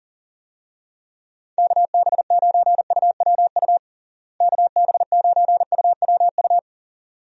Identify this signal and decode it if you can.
Morse code, heard as a keyed tone, about 40 wpm, 700 Hz. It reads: KB9UWU KB9UWU
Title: USA Call Signs - 6 Characters - Repeated 2x - 40wpm